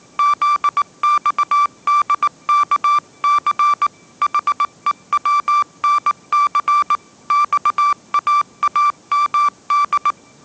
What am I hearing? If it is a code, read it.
Morse code, heard as a keyed tone, about 19 wpm, 1170 Hz. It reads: ZXDKC HEWNC XAAMD
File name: 20wpm